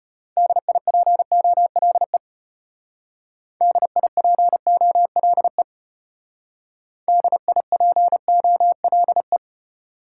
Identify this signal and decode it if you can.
Morse code, heard as a keyed tone, about 33 wpm, 700 Hz. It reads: DIPOLE DIPOLE DIPOLE